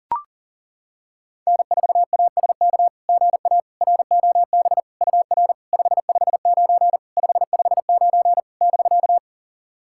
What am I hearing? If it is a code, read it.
Morse code, heard as a keyed tone, about 40 wpm, 690 Hz. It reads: N4ASK GA ROB UR 559 559 <BK>
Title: SOTA QSO Practice 40wpm